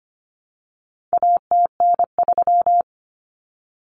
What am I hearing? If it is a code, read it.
Morse code, heard as a keyed tone, about 25 wpm, 700 Hz. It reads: ATN3
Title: Common Cut Numbers - 4-Digit 25wpm